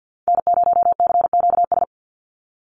Morse code, heard as a keyed tone, about 48 wpm, 700 Hz. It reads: N0CQS
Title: USA Call Signs 50wpm